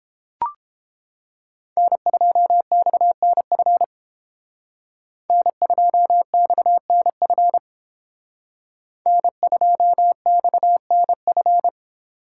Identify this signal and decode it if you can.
Morse code, heard as a keyed tone, about 28 wpm, 700 Hz. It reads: N2XNF N2XNF N2XNF